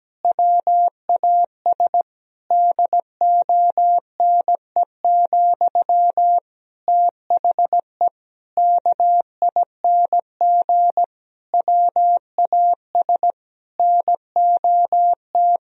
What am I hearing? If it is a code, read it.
Morse code, heard as a keyed tone, about 17 wpm, 700 Hz. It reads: WAS DONE, THE KING WAS NOT
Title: Sentences from Top 1000 Words - Review Part 3 - 17wpm